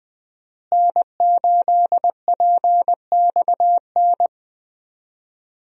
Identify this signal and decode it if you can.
Morse code, heard as a keyed tone, about 20 wpm, 700 Hz. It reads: N8PXN
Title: USA Call Signs - 5 Characters - Repeated 3x - 20wpm